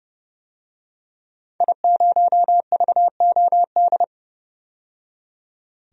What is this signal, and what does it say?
Morse code, heard as a keyed tone, about 30 wpm, 700 Hz. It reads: I0VOD
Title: DXCC-100 Call Signs - 5 Characters - 30wpm